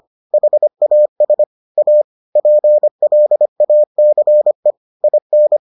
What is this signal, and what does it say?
Morse code, heard as a keyed tone, about 25 wpm, 600 Hz. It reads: HAS A PLACE IN